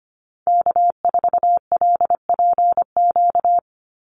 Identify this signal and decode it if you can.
Morse code, heard as a keyed tone, about 25 wpm, 700 Hz. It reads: K4LPQ
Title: NA SOTA Activator Callsigns 25wpm